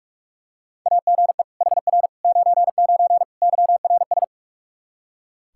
Morse code, heard as a keyed tone, about 43 wpm, 700 Hz. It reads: AGE HR 99 YRS
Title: Practice QSO - Exchange 3 - More Info - 45wpm